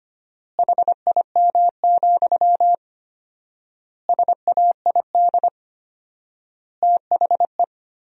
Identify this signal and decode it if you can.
Morse code, heard as a keyed tone, about 25 wpm, 700 Hz. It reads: HIM, SAID THE
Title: 4x Word-Spacing - Sentences from Top 400 Words - 25wpm